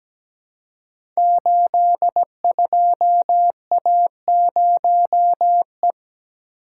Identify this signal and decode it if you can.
Morse code, heard as a keyed tone, about 17 wpm, 700 Hz. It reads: 82A0E
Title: Code Groups - 5 Letter:Numbers - 17wpm